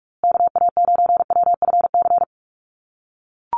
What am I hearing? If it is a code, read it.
Morse code, heard as a keyed tone, about 44 wpm, 700 Hz. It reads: KA9WFC
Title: USA Call Signs Repeated 2x 45wpm